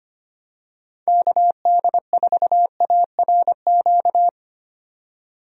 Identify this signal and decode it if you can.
Morse code, heard as a keyed tone, about 25 wpm, 700 Hz. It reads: KD4ARQ
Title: USA Call Signs - 6 Characters - 25wpm